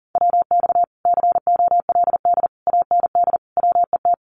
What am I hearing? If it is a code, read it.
Morse code, heard as a keyed tone, about 40 wpm, 700 Hz. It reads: WX COLD AND WET
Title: Common QSO Phrases Encore 40wpm